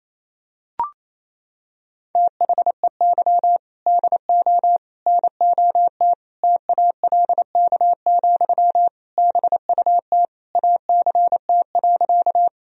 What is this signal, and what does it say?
Morse code, heard as a keyed tone, about 27 wpm, 700 Hz. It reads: THEY DO NOT TALK, BUT ACT.
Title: Sentences from Top 1000 Words - Review Part 1 - 28wpm